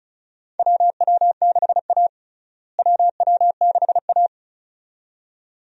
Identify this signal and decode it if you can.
Morse code, heard as a keyed tone, about 36 wpm, 700 Hz. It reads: WW6A WW6A
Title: USA Call Signs - 4 Characters - Repeated 2x - 35wpm